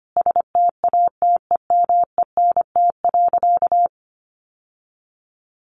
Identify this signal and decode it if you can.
Morse code, heard as a keyed tone, about 25 wpm, 700 Hz. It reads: STATEMENT.
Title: Sentences from Top 1200 Words 25wpm